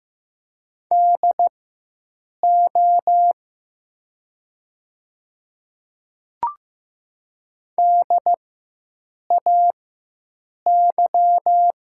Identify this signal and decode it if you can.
Morse code, heard as a keyed tone, about 15 wpm, spaced as for 7 wpm, 700 Hz. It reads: DO DAY
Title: 4x Character Spacing - Top 200 Words - 15wpm